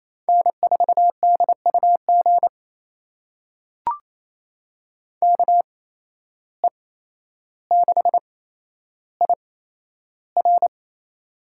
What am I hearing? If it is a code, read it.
Morse code, heard as a keyed tone, about 27 wpm, spaced as for 12 wpm, 700 Hz. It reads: N4DUG KE6IR
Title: USA Call Signs - 5 Characters - 8x Character Spacing - 28wpm